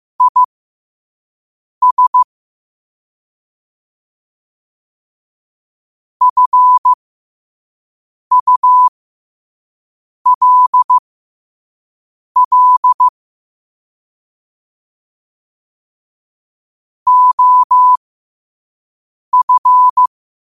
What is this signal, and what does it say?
Morse code, heard as a keyed tone, about 15 wpm, spaced as for 5 wpm, 1000 Hz. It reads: IS FULL OF
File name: qotdfiles / 5wpm